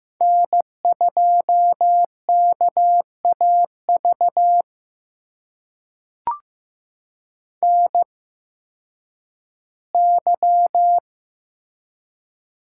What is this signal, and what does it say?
Morse code, heard as a keyed tone, about 15 wpm, spaced as for 10 wpm, 700 Hz. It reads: N2KAV NY
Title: USA Call Signs - 8x Character Spacing - 15wpm